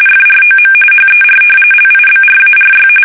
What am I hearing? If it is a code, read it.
RTTY, about 45.45 baud. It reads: FROM HER LIBRARY
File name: fsk850 / battleshipnj